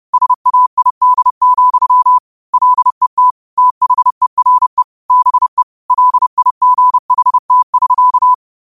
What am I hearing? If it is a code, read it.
Morse code, heard as a keyed tone, about 30 wpm, 1000 Hz. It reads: SAID, LET THERE BE LIGHT<SK>